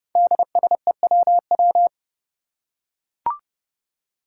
Morse code, heard as a keyed tone, about 30 wpm, 700 Hz. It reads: DSEWW
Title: Five Challenging Letter-Number Combinations 30wpm